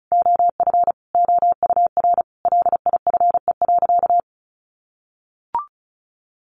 Morse code, heard as a keyed tone, about 34 wpm, 700 Hz. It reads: OF OUR LIFE.
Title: Sentences from Top 1200 Words 35wpm